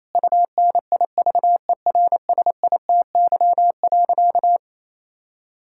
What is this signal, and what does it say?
Morse code, heard as a keyed tone, about 28 wpm, 700 Hz. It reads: UNIVERSITY.
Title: Sentences from Top 1000 Words - Review Part 5 - 28wpm